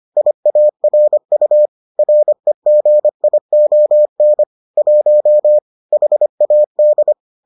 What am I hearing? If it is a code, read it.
Morse code, heard as a keyed tone, about 25 wpm, 600 Hz. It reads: IARU REGION 1 HAD